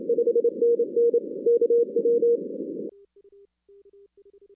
Morse code, heard as a keyed tone, about 27 wpm, 470 Hz. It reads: 5NN KW
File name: 2016 arrl dx cw / 2016-ARRL-DX-CW-K5TR- - 0-42